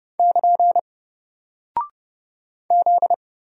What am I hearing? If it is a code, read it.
Morse code, heard as a keyed tone, about 30 wpm, 700 Hz. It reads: <KN> Z
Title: Single Character (Letter:Number:Punctuation:Prosign) Alt 30wpm